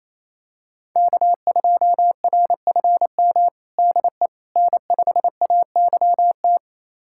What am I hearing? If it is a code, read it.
Morse code, heard as a keyed tone, about 27 wpm, 700 Hz. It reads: K2RFM DE N5AYT
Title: USA Callsign DE Callsign - 5 Characters - 28wpm